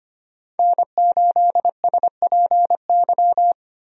Morse code, heard as a keyed tone, about 25 wpm, 700 Hz. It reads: N8SPY
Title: USA Call Signs - 5 Characters - 25wpm